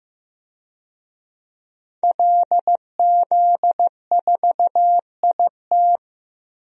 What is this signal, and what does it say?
Morse code, heard as a keyed tone, about 15 wpm, 700 Hz. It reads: LZ4IT
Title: DXCC-100 Call Signs - 5 Characters - Repeated 2x - 15wpm